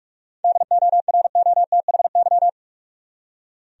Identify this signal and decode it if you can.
Morse code, heard as a keyed tone, about 46 wpm, 700 Hz. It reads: DOROTHY
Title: USA Common Names - 45wpm